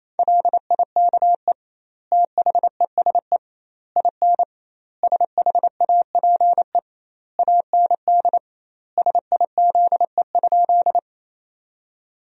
Morse code, heard as a keyed tone, about 28 wpm, 700 Hz. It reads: LIKE THESE IN SHAPE AND SIZE?
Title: 2x Word-Spacing - Sentences from Top 500 Words - 28wpm